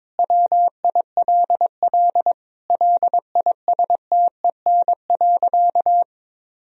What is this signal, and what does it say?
Morse code, heard as a keyed tone, about 22 wpm, 700 Hz. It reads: WILL LISTEN.
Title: Sentences from Top 1000 Words - Review Part 3 - 22wpm